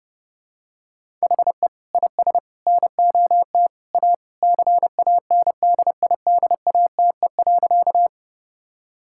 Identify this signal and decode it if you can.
Morse code, heard as a keyed tone, about 32 wpm, 700 Hz. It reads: HE IS NOT A CANDIDATE.
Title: Sentences from Top 1600 Words 30wpm